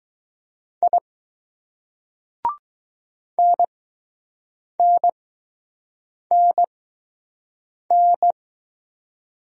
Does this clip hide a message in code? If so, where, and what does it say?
Yes: Morse code from 0.8 to 8.3 s, heard as a keyed tone, about 20 wpm, spaced as for 6 wpm, 700 Hz. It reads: I NNNN